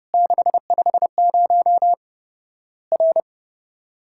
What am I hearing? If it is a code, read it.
Morse code, heard as a keyed tone, about 30 wpm, 700 Hz. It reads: 650 R
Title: Worked All Europe DX Contest - QTC Exchange - 30wpm